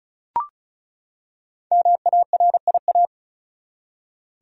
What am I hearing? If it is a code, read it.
Morse code, heard as a keyed tone, about 36 wpm, 700 Hz. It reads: MARIA
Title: USA Common Names - Max 5 Letters - 35wpm